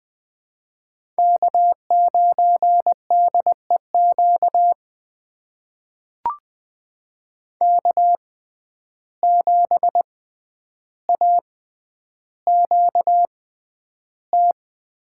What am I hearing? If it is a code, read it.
Morse code, heard as a keyed tone, about 20 wpm, spaced as for 10 wpm, 700 Hz. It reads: K9DEQ K7AQT
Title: USA Call Signs - 5 Characters - 6x Character Spacing - 20wpm